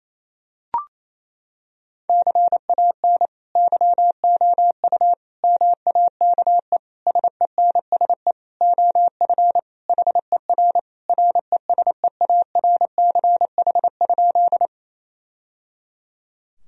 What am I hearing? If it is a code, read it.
Morse code, heard as a keyed tone, about 28 wpm, 700 Hz. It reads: CAN YOU MAKE SENSE OF HER RESEARCH?
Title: Sentences from Top 1000 Words - Review Part 4 - 28wpm